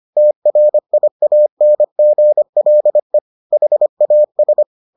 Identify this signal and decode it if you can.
Morse code, heard as a keyed tone, about 25 wpm, 600 Hz. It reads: TRIANGLE HAS